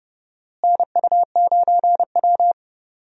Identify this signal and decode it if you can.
Morse code, heard as a keyed tone, about 29 wpm, 700 Hz. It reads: NU9W
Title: USA Call Signs - 4 Characters - Repeated 3x - 30wpm